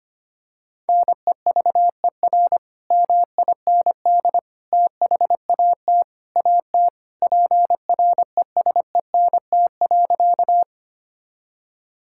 Morse code, heard as a keyed tone, about 25 wpm, 700 Hz. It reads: NEVER MIND THAT AT PRESENT.
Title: Sentences from Top 1000 Words - Review Part 5 - 25wpm